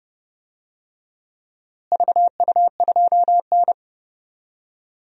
Morse code, heard as a keyed tone, about 30 wpm, 700 Hz. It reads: VU2N
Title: DXCC-100 Call Signs - 4 Characters - Repeated 3x - 30wpm